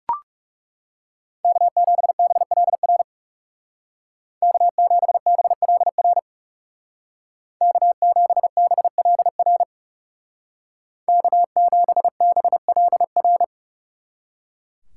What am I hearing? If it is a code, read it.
Morse code, heard as a keyed tone, about 39 wpm, 700 Hz. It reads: K7BLR K7BLR K7BLR K7BLR